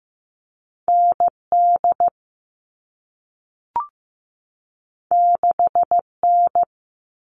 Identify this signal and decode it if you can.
Morse code, heard as a keyed tone, about 15 wpm, 700 Hz. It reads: ND 6N
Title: All Cut Numbers - 2-Digit 15wpm